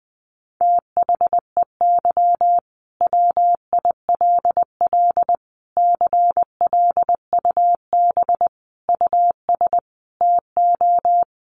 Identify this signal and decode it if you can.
Morse code, heard as a keyed tone, about 20 wpm, 700 Hz. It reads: THEY WILL CLUB US TO
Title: Sentences from Top 1200 Words 20wpm